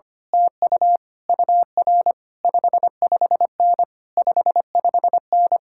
Morse code, heard as a keyed tone, about 24 wpm, 700 Hz. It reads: TU UR 55N 55N
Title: Speed-Racing Common QSO Phrases - 25wpm